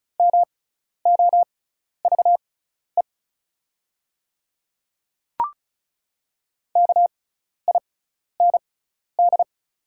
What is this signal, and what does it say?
Morse code, heard as a keyed tone, about 36 wpm, spaced as for 12 wpm, 700 Hz. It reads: MOVE KIND
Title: 6x Character Spacing - Top 200 Words - 35wpm